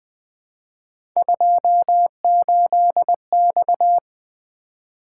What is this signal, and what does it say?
Morse code, heard as a keyed tone, about 20 wpm, 700 Hz. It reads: 28X
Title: Three Letter-Number Combinations - 20wpm